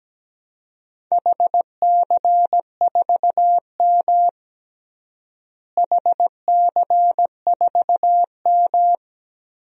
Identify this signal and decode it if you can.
Morse code, heard as a keyed tone, about 17 wpm, 700 Hz. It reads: HC4M HC4M
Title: DXCC-100 Call Signs - 4 Characters - Repeated 3x - 17wpm